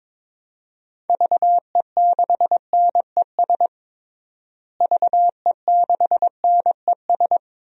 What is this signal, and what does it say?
Morse code, heard as a keyed tone, about 22 wpm, 700 Hz. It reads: VE6NES VE6NES
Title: DXCC-100 Call Signs - 6 Characters - Repeated 3x - 22wpm